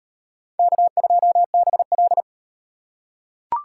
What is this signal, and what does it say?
Morse code, heard as a keyed tone, about 36 wpm, 700 Hz. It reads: K2BL